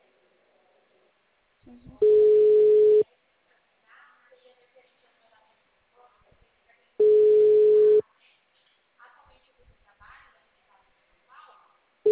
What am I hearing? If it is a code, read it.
Ringback tone (Europe, 425 Hz), with speech.